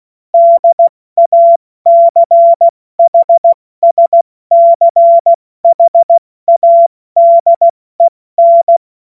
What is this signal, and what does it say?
Morse code, heard as a keyed tone, about 16 wpm, 680 Hz. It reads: DACHSCHADEN
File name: Morsekurs / Witze / Witz 07. September 2021, 70 F80